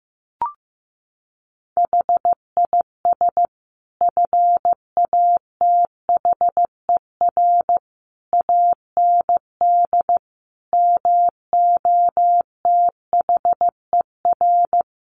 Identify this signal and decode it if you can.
Morse code, heard as a keyed tone, about 15 wpm, 700 Hz. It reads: HIS FATHER AND MOTHER
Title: Sentences from Top 1500 Words 15wpm